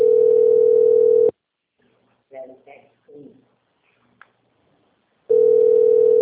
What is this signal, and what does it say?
Ringback tone (North America), 440 + 480 Hz, with speech.
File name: out- - - - - .2617